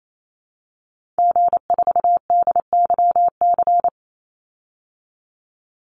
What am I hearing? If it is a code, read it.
Morse code, heard as a keyed tone, about 28 wpm, 700 Hz. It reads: G4DYC
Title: Call Signs Repeated 3x - 28wpm